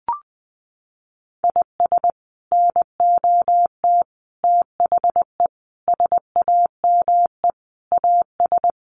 Morse code, heard as a keyed tone, about 20 wpm, 700 Hz. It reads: IS NOT THE SAME AS
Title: Sets of 5 Words Encore 20wpm